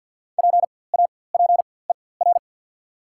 Morse code, heard as a keyed tone, about 48 wpm, spaced as for 23 wpm, 700 Hz. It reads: PAPER
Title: 4x Character Spacing - Top 300 Words - 50wpm